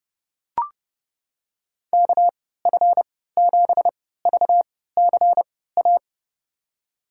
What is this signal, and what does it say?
Morse code, heard as a keyed tone, about 29 wpm, spaced as for 17 wpm, 700 Hz. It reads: KF7VCA
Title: USA Call Signs - 3x Character Spacing - 30wpm